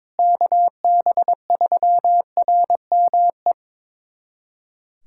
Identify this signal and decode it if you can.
Morse code, heard as a keyed tone, about 22 wpm, 700 Hz. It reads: KB3RME
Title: USA Callsign DE Callsign - 6 Characters - 22wpm